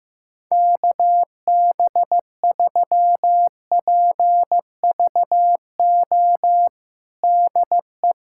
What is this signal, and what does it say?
Morse code, heard as a keyed tone, about 16 wpm, 700 Hz. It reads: KB3PVO DE
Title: USA Callsign DE Callsign - 6 Characters - 15wpm